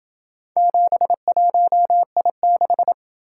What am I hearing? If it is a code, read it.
Morse code, heard as a keyed tone, about 28 wpm, 700 Hz. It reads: 71I6